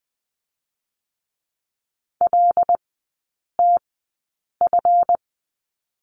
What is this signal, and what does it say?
Morse code, heard as a keyed tone, about 20 wpm, spaced as for 8 wpm, 700 Hz. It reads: LTF
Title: 138 Lesson 23e - Letter f - Three Characters 20wpm